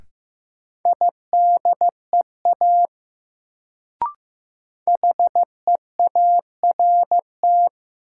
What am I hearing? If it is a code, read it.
Morse code, heard as a keyed tone, about 15 wpm, 700 Hz. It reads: IDEA HEART
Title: Sets of 3 Words - New Word Drill - 15wpm